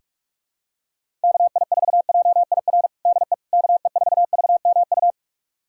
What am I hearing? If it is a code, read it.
Morse code, heard as a keyed tone, about 41 wpm, 700 Hz. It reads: KI4JIR DE KE4VMU
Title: USA Callsign DE Callsign - 6 Characters - 45wpm